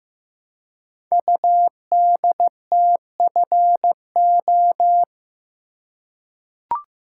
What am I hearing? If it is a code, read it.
Morse code, heard as a keyed tone, about 15 wpm, 700 Hz. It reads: UDTFO
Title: Code Groups - 5 Letters - 15wpm